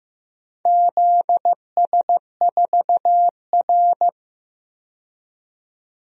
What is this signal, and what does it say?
Morse code, heard as a keyed tone, about 15 wpm, 700 Hz. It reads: ZS4R
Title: DXCC-100 Call Signs - 4 Characters - 15wpm